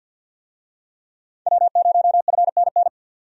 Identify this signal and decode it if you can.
Morse code, heard as a keyed tone, about 47 wpm, 700 Hz. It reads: W0FNN
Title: USA Call Signs - 5 Characters - 50wpm